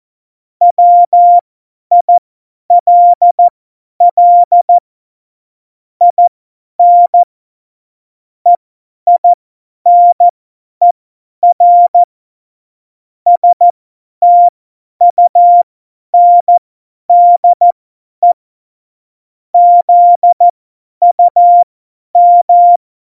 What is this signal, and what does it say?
Morse code, heard as a keyed tone, about 14 wpm, spaced as for 10 wpm, 700 Hz. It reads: WILL IN EINER STUNDE ZUM
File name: Morsekurs / Witze / Witz 08. September 2020, 50 F70